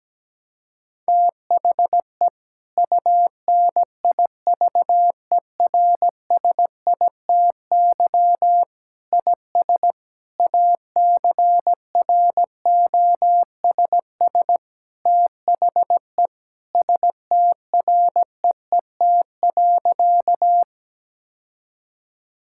Morse code, heard as a keyed tone, about 17 wpm, 700 Hz. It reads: THE UNIVERSITY IS ACROSS THE STREET.
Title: Sentences from Top 1000 Words - Review Part 5 - 17wpm